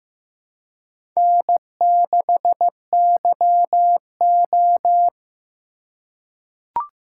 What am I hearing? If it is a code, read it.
Morse code, heard as a keyed tone, about 15 wpm, 700 Hz. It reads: N6YO
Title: USA Call Signs - 4 Characters - 3x Character Spacing - 15wpm